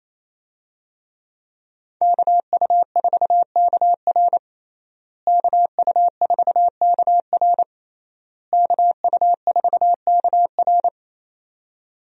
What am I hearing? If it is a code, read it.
Morse code, heard as a keyed tone, about 28 wpm, 700 Hz. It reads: KU4KR KU4KR KU4KR
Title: USA Call Signs - 5 Characters - Repeated 3x - 28wpm